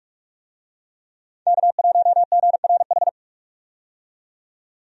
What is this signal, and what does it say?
Morse code, heard as a keyed tone, about 46 wpm, 700 Hz. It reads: K1GRH
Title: USA Call Signs - 5 Characters - 45wpm